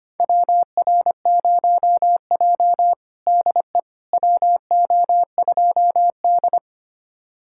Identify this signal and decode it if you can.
Morse code, heard as a keyed tone, about 25 wpm, 700 Hz. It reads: WR0J DE WO2D
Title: USA Callsign DE Callsign - 4 Characters - 25wpm